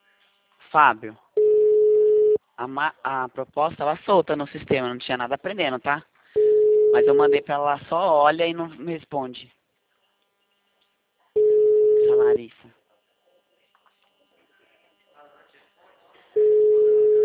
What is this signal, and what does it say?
Ringback tone (Europe, 425 Hz), with speech.